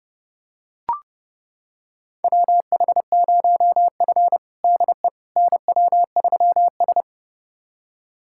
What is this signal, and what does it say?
Morse code, heard as a keyed tone, about 32 wpm, 700 Hz. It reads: WH0F DE NW3S
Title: USA Callsign DE Callsign - 4 Characters - 30wpm